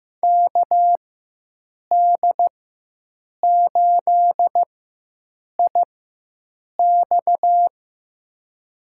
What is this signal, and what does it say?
Morse code, heard as a keyed tone, about 15 wpm, spaced as for 7 wpm, 700 Hz. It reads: KD8IX
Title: USA Call Signs - 4x Character Spacing - 15wpm